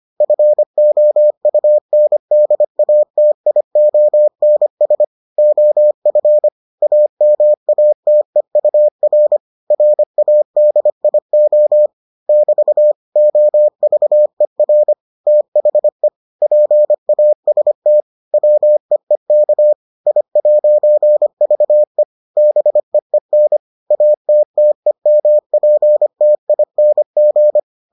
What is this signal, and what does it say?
Morse code, heard as a keyed tone, about 25 wpm, 600 Hz. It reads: FOUNDATIONS OF AMATEUR RADIO <BT> OVER THE PAST WEEK I'VE BEEN ATTEMPTING